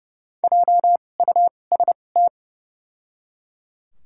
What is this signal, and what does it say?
Morse code, heard as a keyed tone, about 30 wpm, spaced as for 22 wpm, 700 Hz. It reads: JUST
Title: 2x Character Spacing - Top 200 Words - 30wpm